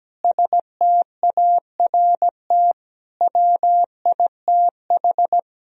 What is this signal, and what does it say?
Morse code, heard as a keyed tone, about 17 wpm, 700 Hz. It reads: START WITH
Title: Sentences from Top 1000 Words - Review Part 1 - 17wpm